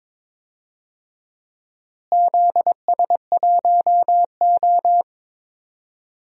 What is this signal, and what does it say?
Morse code, heard as a keyed tone, about 22 wpm, 700 Hz. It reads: ZS1O
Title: DXCC-100 Call Signs - 4 Characters - Repeated 3x - 22wpm